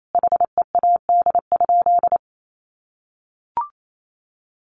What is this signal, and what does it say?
Morse code, heard as a keyed tone, about 28 wpm, 700 Hz. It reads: HEAD?
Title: Sentences from Top 600 Words 28wpm